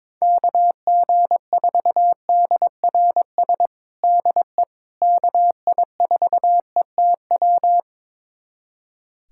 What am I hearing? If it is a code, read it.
Morse code, heard as a keyed tone, about 22 wpm, 700 Hz. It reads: KG4DRS DE KI4ETW
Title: USA Callsign DE Callsign - 6 Characters - 22wpm